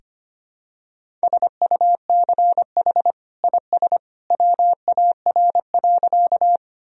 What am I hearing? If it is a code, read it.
Morse code, heard as a keyed tone, about 25 wpm, 700 Hz. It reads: SUCH IS WAR.
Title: Sentences from Top 1000 Words - Review Part 2 - 25wpm